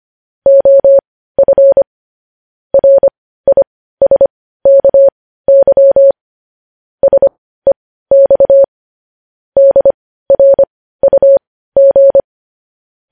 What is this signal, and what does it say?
Morse code, heard as a keyed tone, about 25 wpm, spaced as for 15 wpm, 550 Hz. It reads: OF RISKY SEX DRUG
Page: News Headlines in Morse Code at 15 WPM